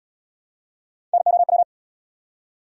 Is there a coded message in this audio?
Yes: Morse code from 1.1 to 1.6 s, heard as a keyed tone, about 70 wpm, 700 Hz. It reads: NOW